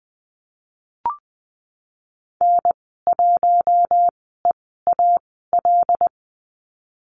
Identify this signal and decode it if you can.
Morse code, heard as a keyed tone, about 20 wpm, spaced as for 15 wpm, 700 Hz. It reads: N1EAL
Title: 076 Lesson 13g - Letter L - Call Signs 20wpm